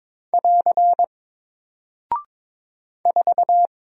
Morse code, heard as a keyed tone, about 22 wpm, 700 Hz. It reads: <AR> 4
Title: Single Character (Letter:Number:Punctuation:Prosign) 22wpm